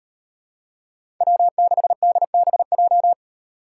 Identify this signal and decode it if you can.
Morse code, heard as a keyed tone, about 38 wpm, 700 Hz. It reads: W6DBJ